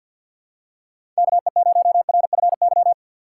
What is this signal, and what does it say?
Morse code, heard as a keyed tone, about 49 wpm, 700 Hz. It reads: KE0RFY
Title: USA Call Signs - 6 Characters - 3x Character Spacing - 50wpm